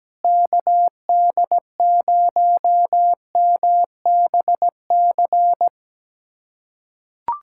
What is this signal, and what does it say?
Morse code, heard as a keyed tone, about 17 wpm, 700 Hz. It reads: KD0MBC
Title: USA Callsign DE Callsign - 6 Characters - 17wpm